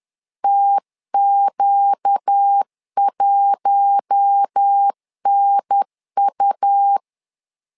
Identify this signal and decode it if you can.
Morse code, heard as a keyed tone, about 10 wpm, 800 Hz. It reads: TQ1NU
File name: LEZIONI / 4 Q1 10wpm 2448